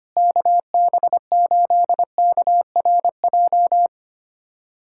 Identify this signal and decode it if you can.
Morse code, heard as a keyed tone, about 25 wpm, 700 Hz. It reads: KB8KRJ
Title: USA Callsign DE Callsign 25wpm